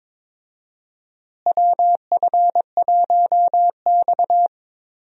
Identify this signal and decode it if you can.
Morse code, heard as a keyed tone, about 22 wpm, 700 Hz. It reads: WF1X
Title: USA Call Signs - 4 Characters - Repeated 2x - 22wpm